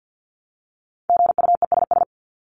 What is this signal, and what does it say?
Morse code, heard as a keyed tone, about 50 wpm, 700 Hz. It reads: GUESS
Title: Top 1251-1500 Words 50wpm